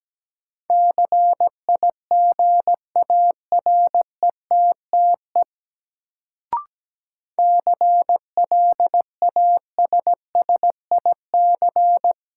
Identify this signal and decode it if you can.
Morse code, heard as a keyed tone, about 17 wpm, 700 Hz. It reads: CIGARETTE CLASSIC
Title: Sentences from Top 1000 Words - New Word Drill - 17wpm